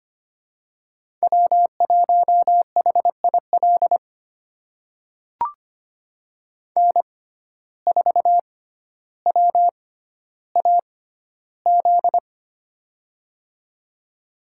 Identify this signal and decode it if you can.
Morse code, heard as a keyed tone, about 25 wpm, spaced as for 13 wpm, 690 Hz. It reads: W1HIL N4WAZ
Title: USA Call Signs - 5 Characters - 6x Character Spacing - 25wpm